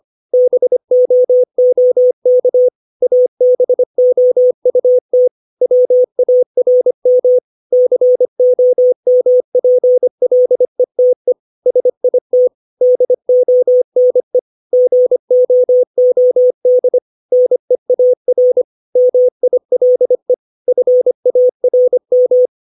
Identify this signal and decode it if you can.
Morse code, heard as a keyed tone, about 25 wpm, 500 Hz. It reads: BOOK ABOUT WARM COMPLETE SIT DONE GOOD NEAR MILE FARM
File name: 632 words / 632 Words 25WPM